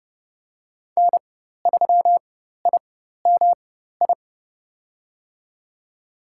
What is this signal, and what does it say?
Morse code, heard as a keyed tone, about 29 wpm, spaced as for 14 wpm, 700 Hz. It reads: N3IMI
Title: USA Call Signs - 5 Characters - 4x Character Spacing - 30wpm